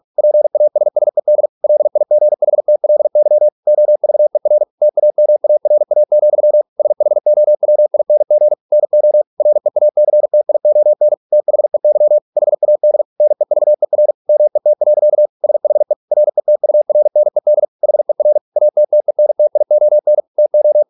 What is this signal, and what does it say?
Morse code, heard as a keyed tone, about 46 wpm, 600 Hz. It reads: PASSED LIGHTLY OVER TAMARA, SHOWING NO REACTION THEY HAD NEVER MET. SHE RETURNED HER ATTENTION TO